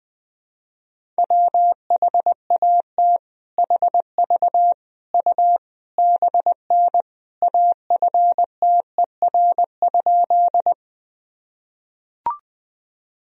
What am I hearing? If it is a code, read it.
Morse code, heard as a keyed tone, about 20 wpm, 700 Hz. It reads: WHAT HV U BN AFTER?
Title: Sentences from Top 200 Words and Abbreviations 20wpm